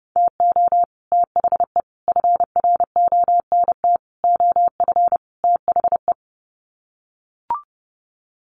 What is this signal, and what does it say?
Morse code, heard as a keyed tone, about 30 wpm, 700 Hz. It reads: TO THE FRONT OF THE
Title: Sets of 5 Words 30wpm